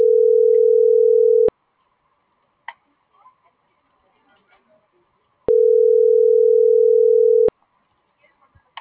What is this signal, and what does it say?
Ringback tone (North America), 440 + 480 Hz.
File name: CENTRAL GENERAL / out karla.sky - .19651